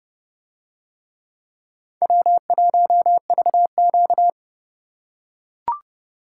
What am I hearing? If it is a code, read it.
Morse code, heard as a keyed tone, about 30 wpm, 700 Hz. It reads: W1VQ